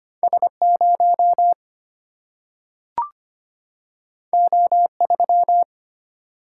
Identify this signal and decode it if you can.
Morse code, heard as a keyed tone, about 25 wpm, 700 Hz. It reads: S0 O3
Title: Single-Letter Single-Number 25wpm